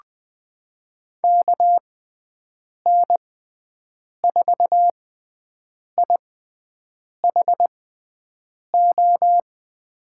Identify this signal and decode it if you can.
Morse code, heard as a keyed tone, about 20 wpm, spaced as for 7 wpm, 700 Hz. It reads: KN4IHO
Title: USA Call Signs - 6x Character Spacing - 20wpm